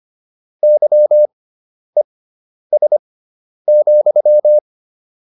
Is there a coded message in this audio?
Yes: Morse code from 0.6 to 4.6 s, heard as a keyed tone, about 25 wpm, spaced as for 10 wpm, 600 Hz. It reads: YES,